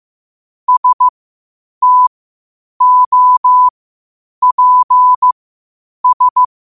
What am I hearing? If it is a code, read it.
Morse code, heard as a keyed tone, about 15 wpm, spaced as for 8 wpm, 1000 Hz. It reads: STOPS
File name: qotdfiles / 8wpm